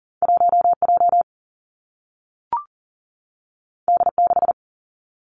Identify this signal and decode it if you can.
Morse code, heard as a keyed tone, about 39 wpm, 700 Hz. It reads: 1J D6
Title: Two Challenging Letter-Number Combinations 40wpm